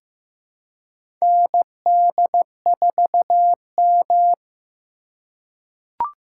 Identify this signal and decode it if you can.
Morse code, heard as a keyed tone, about 15 wpm, 700 Hz. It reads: ND4M
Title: USA Call Signs - 2x Character Spacing - 15wpm